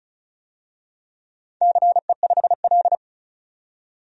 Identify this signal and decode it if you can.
Morse code, heard as a keyed tone, about 36 wpm, 700 Hz. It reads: CE5L
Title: DXCC-100 Call Signs - 4 Characters - 35wpm